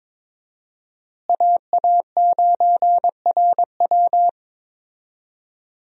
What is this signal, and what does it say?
Morse code, heard as a keyed tone, about 23 wpm, 700 Hz. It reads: AA9RW
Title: USA Call Signs - 5 Characters - Repeated 3x - 22wpm